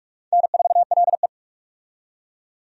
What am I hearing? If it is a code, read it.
Morse code, heard as a keyed tone, about 46 wpm, 700 Hz. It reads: N4LE
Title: USA Call Signs - 4 Characters - 45wpm